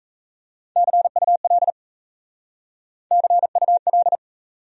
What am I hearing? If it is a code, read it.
Morse code, heard as a keyed tone, about 42 wpm, 700 Hz. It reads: CUL CUL